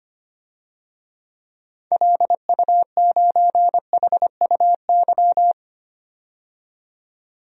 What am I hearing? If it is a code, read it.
Morse code, heard as a keyed tone, about 25 wpm, 700 Hz. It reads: LU9HUY
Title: DXCC-100 Call Signs - 6 Characters - 25wpm